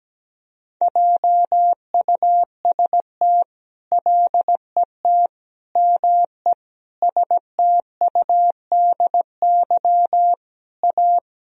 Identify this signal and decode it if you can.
Morse code, heard as a keyed tone, about 17 wpm, 700 Hz. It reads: JUST LET ME STUDY A
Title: Sentences from Top 1000 Words - Review Part 1 - 17wpm